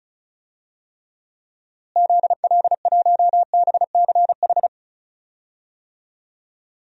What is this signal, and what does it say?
Morse code, heard as a keyed tone, about 36 wpm, 700 Hz. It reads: ZL1BCH
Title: DXCC-100 Call Signs - 6 Characters - Repeated 2x - 35wpm